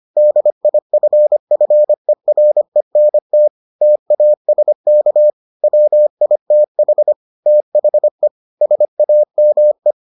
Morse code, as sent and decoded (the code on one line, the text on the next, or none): -.. .. ..-. ..-. .-. -. - / - .- ... -.- / .-- .. - .... / - .... / ... .- -- .
DIFFERENT TASK WITH THE SAME